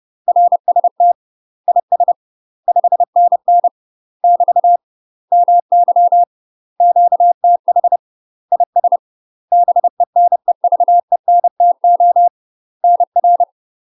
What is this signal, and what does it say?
Morse code, heard as a keyed tone, about 30 wpm, 700 Hz. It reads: RST IS 5NN <BT> MY QTH IS BENEVENTO NR
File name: UK MASTER QSOs TRAINING / 30WPM / UK MASTER QSOs 0041